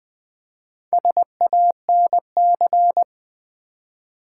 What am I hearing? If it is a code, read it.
Morse code, heard as a keyed tone, about 20 wpm, 700 Hz. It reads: SANC
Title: Build A Word - 4 Letters At A Time - 8-Letter Words - 20wpm